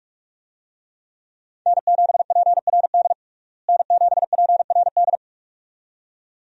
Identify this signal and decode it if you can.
Morse code, heard as a keyed tone, about 47 wpm, 700 Hz. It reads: N7PRD N7PRD
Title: USA Call Signs - 5 Characters - Repeated 2x - 45wpm